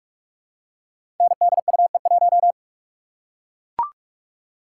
Morse code, heard as a keyed tone, about 44 wpm, 700 Hz. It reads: NDUE1
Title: Code Groups - 5 Letter:Numbers - 45wpm